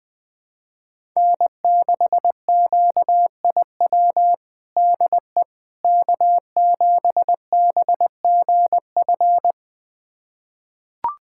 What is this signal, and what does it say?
Morse code, heard as a keyed tone, about 21 wpm, 700 Hz. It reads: N6QIW DE K7BGF
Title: USA Callsign DE Callsign - 5 Characters - 20wpm